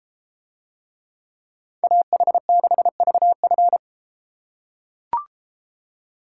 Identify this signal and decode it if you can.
Morse code, heard as a keyed tone, about 33 wpm, 700 Hz. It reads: AH6VF